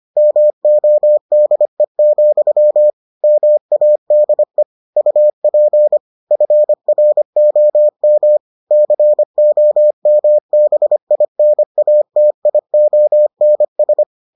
Morse code, heard as a keyed tone, about 25 wpm, 600 Hz. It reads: MODE, MADE UP FROM COMBINATIONS